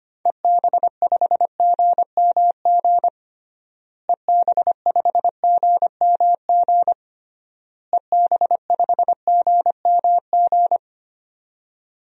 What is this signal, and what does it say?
Morse code, heard as a keyed tone, about 25 wpm, 700 Hz. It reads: EB5GMG EB5GMG EB5GMG
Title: DXCC-100 Call Signs - 6 Characters - Repeated 3x - 25wpm